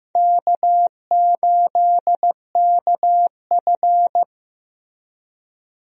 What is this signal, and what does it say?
Morse code, heard as a keyed tone, about 15 wpm, 700 Hz. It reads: K8KF
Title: USA Call Signs - 4 Characters - Repeated 2x - 15wpm